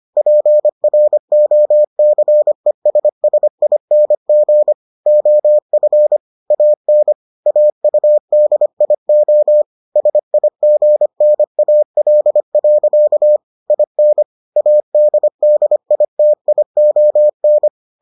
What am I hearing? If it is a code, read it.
Morse code, heard as a keyed tone, about 25 wpm, 600 Hz. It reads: PROCESSING OF AN AUDIO SIGNAL. IN ADDITION